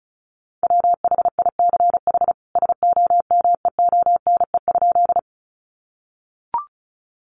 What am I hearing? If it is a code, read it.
Morse code, heard as a keyed tone, about 34 wpm, 700 Hz. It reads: WHICH SOMEONE?
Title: Sentences from Top 600 Words 35wpm